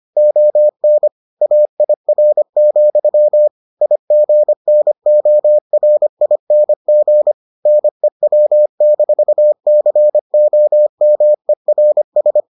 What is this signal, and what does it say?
Morse code, heard as a keyed tone, about 25 wpm, 600 Hz. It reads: ON AIR, IGNORING NEW-COMERS